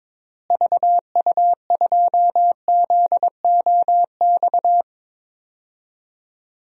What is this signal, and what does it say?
Morse code, heard as a keyed tone, about 22 wpm, 700 Hz. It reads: VU2ZOX
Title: DXCC-100 Call Signs - 6 Characters - Repeated 2x - 22wpm